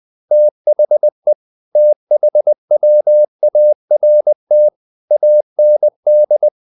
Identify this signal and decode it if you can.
Morse code, heard as a keyed tone, about 20 wpm, 600 Hz. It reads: THE THWART AND
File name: the old man and the sea / the old man and the sea part 1 20wpm